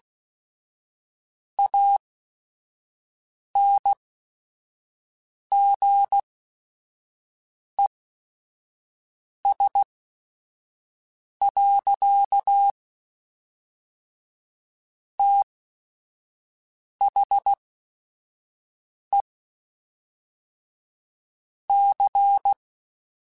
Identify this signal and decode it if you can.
Morse code, heard as a keyed tone, about 16 wpm, spaced as for 5 wpm, 800 Hz. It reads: ANGES. THE C
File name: Technical Subjects / Technical Subjects-mp3s / XC @5